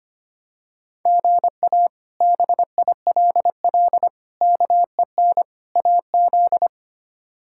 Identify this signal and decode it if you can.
Morse code, heard as a keyed tone, about 25 wpm, 700 Hz. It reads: GA BILL KEN AZ
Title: K1USN SST QSO Practice - 25wpm